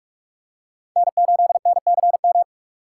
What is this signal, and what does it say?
Morse code, heard as a keyed tone, about 46 wpm, 700 Hz. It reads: N8NCM
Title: USA Call Signs - 5 Characters - 6x Character Spacing - 45wpm